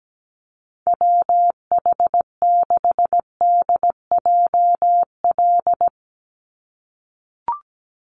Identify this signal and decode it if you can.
Morse code, heard as a keyed tone, about 17 wpm, 700 Hz. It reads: WH6DJL
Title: USA Call Signs 17wpm